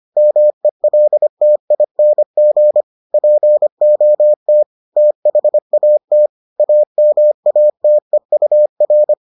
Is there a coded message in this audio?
Yes: Morse code from 0.2 to 9.1 s, heard as a keyed tone, about 25 wpm, 600 Hz. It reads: MELTING POT THAT AMATEUR